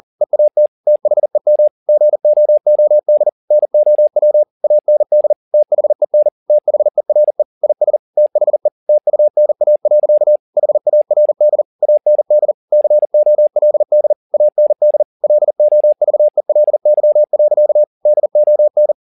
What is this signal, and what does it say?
Morse code, heard as a keyed tone, about 40 wpm, 600 Hz. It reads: EAT THEM GOOD NOW AND THEN THERE IS THE TUNA. HARD AND COLD AND LOVELY. DON